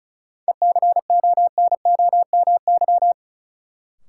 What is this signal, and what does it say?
Morse code, heard as a keyed tone, about 36 wpm, 700 Hz. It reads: ECONOMY
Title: Sentences from Top 1200 Words - New Word Drill Alt - 35wpm